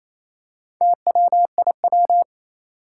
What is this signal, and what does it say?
Morse code, heard as a keyed tone, about 27 wpm, 700 Hz. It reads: TWIW
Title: Four Challenging Letter-Number Combinations 28wpm